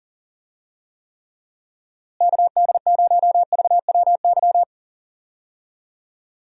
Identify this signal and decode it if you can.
Morse code, heard as a keyed tone, about 40 wpm, 700 Hz. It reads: KD0VWY
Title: USA Call Signs - 6 Characters - Repeated 2x - 40wpm